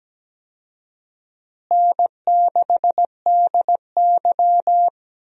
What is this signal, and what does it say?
Morse code, heard as a keyed tone, about 17 wpm, 700 Hz. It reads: N6DY
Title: USA Call Signs - 4 Characters - Repeated 3x - 17wpm